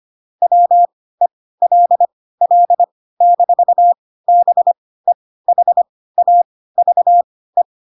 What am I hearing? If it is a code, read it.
Morse code, heard as a keyed tone, about 25 wpm, spaced as for 16 wpm, 700 Hz. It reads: WELL-BEHAVE
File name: Pinocchio / Pinocchio ch3 10 25 ik1hge